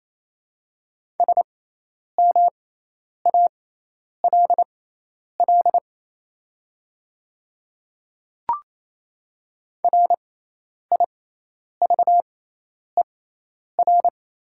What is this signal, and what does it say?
Morse code, heard as a keyed tone, about 28 wpm, spaced as for 10 wpm, 690 Hz. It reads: SMALL RIVER
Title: 6x Character Spacing - Top 300 Words - 28wpm